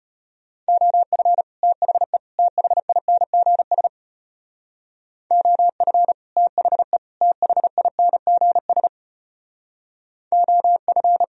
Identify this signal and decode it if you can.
Morse code, heard as a keyed tone, about 30 wpm, 700 Hz. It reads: OF THE THINGS OF THE THINGS OF
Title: Speed-Racing - Sets of 3 Words 25wpm